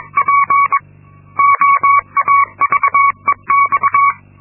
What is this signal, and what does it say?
Morse code, heard as a keyed tone, about 22 wpm, 1120 Hz. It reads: P OAVEX
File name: Linksmakalnis-2